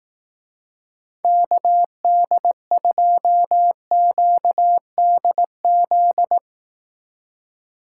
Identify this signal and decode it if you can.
Morse code, heard as a keyed tone, about 18 wpm, 700 Hz. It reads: KD2QDZ
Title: Speed Racing - USA 2x3 Call Signs 15wpm